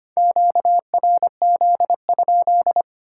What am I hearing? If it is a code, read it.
Morse code, heard as a keyed tone, about 25 wpm, 690 Hz. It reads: QRZ?
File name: Top 90 QSO Elements - 3x Character Spacing - 25wpm